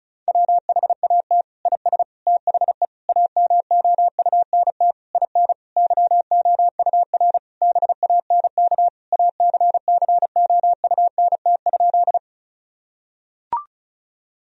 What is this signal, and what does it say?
Morse code, heard as a keyed tone, about 36 wpm, 700 Hz. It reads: WHAT IS THE AMOUNT IN YOUR BANK ACCOUNT?
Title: Sentences from Top 1000 Words - Review Part 4 - 35wpm